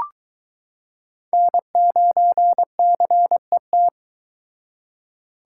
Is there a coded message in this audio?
Yes: Morse code from 1.3 to 3.9 s, heard as a keyed tone, about 22 wpm, 700 Hz. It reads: N9CET